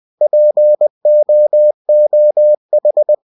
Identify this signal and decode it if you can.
Morse code, heard as a keyed tone, about 20 wpm, 600 Hz. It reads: POOH
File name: cw-books / winnie-the-pooh / winnie-20wpm-page-0099